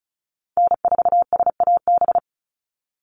Morse code, heard as a keyed tone, about 34 wpm, 700 Hz. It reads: N4SAB
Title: USA Call Signs 35wpm